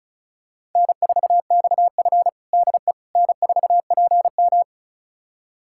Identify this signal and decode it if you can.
Morse code, heard as a keyed tone, about 36 wpm, 700 Hz. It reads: N4XF DE N4PM
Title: USA Callsign DE Callsign - 4 Characters - 35wpm